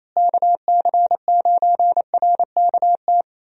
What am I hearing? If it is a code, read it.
Morse code, heard as a keyed tone, about 28 wpm, 700 Hz. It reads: KC9RKT
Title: USA Call Signs - 6 Characters - Repeated 2x - 28wpm